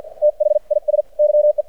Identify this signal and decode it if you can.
Morse code, heard as a keyed tone, about 52 wpm, spaced as for 43 wpm, 600 Hz. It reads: THIS YE
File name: archived / CRSnet